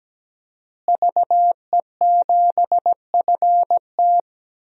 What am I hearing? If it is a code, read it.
Morse code, heard as a keyed tone, about 17 wpm, 700 Hz. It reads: VE7FT
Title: DXCC-100 Call Signs - 5 Characters - Repeated 2x - 17wpm